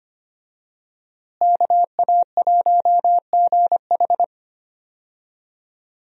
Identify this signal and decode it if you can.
Morse code, heard as a keyed tone, about 25 wpm, 700 Hz. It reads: KA1GH
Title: USA Call Signs - 5 Characters - Repeated 3x - 25wpm